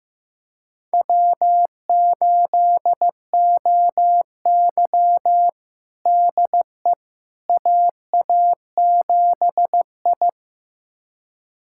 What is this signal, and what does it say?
Morse code, heard as a keyed tone, about 15 wpm, 700 Hz. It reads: W8OY DE AA7I
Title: USA Callsign DE Callsign - 4 Characters - 15wpm